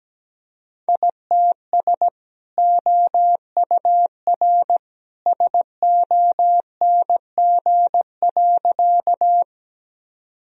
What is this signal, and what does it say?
Morse code, heard as a keyed tone, about 17 wpm, 700 Hz. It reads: ITS OUR SONG.
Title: Sentences from Top 1000 Words - Review Part 2 - 17wpm